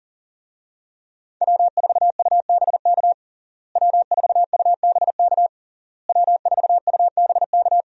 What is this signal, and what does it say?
Morse code, heard as a keyed tone, about 40 wpm, 700 Hz. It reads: W4UBK W4UBK W4UBK
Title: USA Call Signs - 5 Characters - Repeated 3x - 40wpm